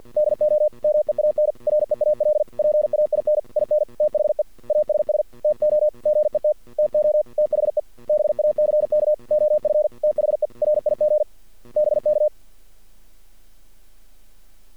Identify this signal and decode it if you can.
Morse code, heard as a keyed tone, about 50 wpm, 610 Hz. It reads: DO MEAN REAL PAIN IN THE ASS TO GET TO THE BACK OF THE RIG BK
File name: archived / CRSnet